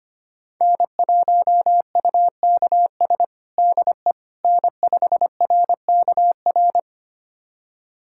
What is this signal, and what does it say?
Morse code, heard as a keyed tone, about 25 wpm, 700 Hz. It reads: N1UKS DE N5RKR
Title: USA Callsign DE Callsign - 5 Characters - 25wpm